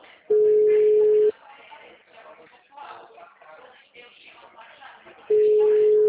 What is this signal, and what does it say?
Ringback tone (Europe, 425 Hz).